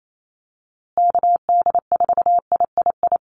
Morse code, heard as a keyed tone, about 28 wpm, 700 Hz. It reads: KD4III
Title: USA Call Signs Repeated 2x 28wpm